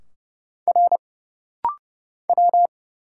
Morse code, heard as a keyed tone, about 30 wpm, 700 Hz. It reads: R W
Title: Single Letter-Number Alt - Rapid-Fire 30wpm